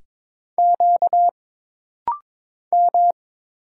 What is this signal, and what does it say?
Morse code, heard as a keyed tone, about 22 wpm, 700 Hz. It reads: Q M
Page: This rapid-fire timing is geared to push your ICR (Instant Character Recognition) skills.